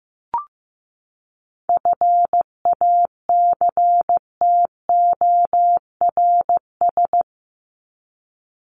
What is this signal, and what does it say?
Morse code, heard as a keyed tone, about 15 wpm, 700 Hz. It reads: FACTORS
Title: ES:S Suffix - Top 100 Words with Base Words 15wpm